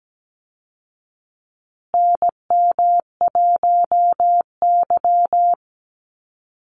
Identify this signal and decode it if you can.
Morse code, heard as a keyed tone, about 17 wpm, 700 Hz. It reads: NM1Y
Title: Call Signs Repeated 2x - 17wpm